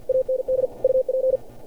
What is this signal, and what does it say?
Morse code, heard as a keyed tone, about 47 wpm, 520 Hz. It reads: AND UP
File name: archived / CRSnet